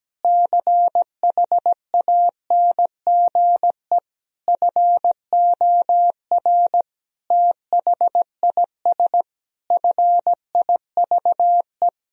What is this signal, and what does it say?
Morse code, heard as a keyed tone, about 17 wpm, 700 Hz. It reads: CHANGE FOR THIS FIVE
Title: Sentences from Top 1000 Words - Review Part 2 - 17wpm